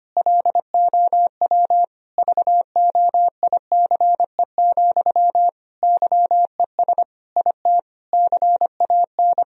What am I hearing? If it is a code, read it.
Morse code, heard as a keyed tone, about 25 wpm, 700 Hz. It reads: LOW VOICE, YES IT CAN
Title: Sentences from Top 1000 Words - Review Part 1 - 25wpm